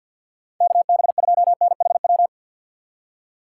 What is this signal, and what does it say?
Morse code, heard as a keyed tone, about 47 wpm, 700 Hz. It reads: KB2NHW
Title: USA Call Signs - 6 Characters - 6x Character Spacing - 50wpm